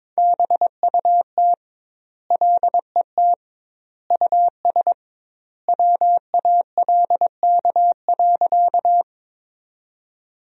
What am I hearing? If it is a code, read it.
Morse code, heard as a keyed tone, about 22 wpm, 700 Hz. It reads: BUT LET US WALK.
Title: 2x Word-Spacing - Sentences from Top 300 Words - 22wpm